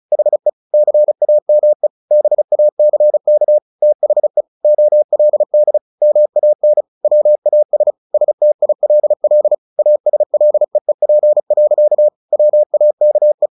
Morse code, heard as a keyed tone, about 34 wpm, 600 Hz. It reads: HE CAME BACK THE OLD MAN WAS STILL ASLEEP. WAKE
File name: the old man and the sea / the old man and the sea 1 35wpm